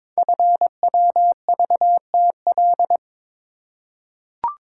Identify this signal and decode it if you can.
Morse code, heard as a keyed tone, about 22 wpm, 700 Hz. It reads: FWVTL
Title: Code Groups - 5 Letter:Numbers - 22wpm